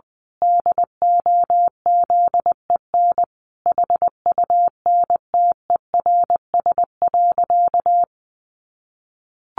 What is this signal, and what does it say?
Morse code, heard as a keyed tone, about 20 wpm, 700 Hz. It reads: DOZEN HUNTERS.
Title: A Sound Of Thunder 20wpm